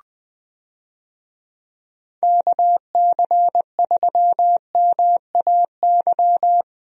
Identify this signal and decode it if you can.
Morse code, heard as a keyed tone, about 21 wpm, 700 Hz. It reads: KC3MAY
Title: USA Call Signs - 6 Characters - Repeated 3x - 20wpm